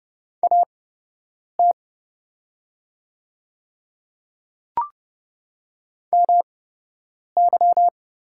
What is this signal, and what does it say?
Morse code, heard as a keyed tone, about 30 wpm, spaced as for 8 wpm, 700 Hz. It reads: AT MY
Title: 8x Character Spacing - Top 100 Words - 30wpm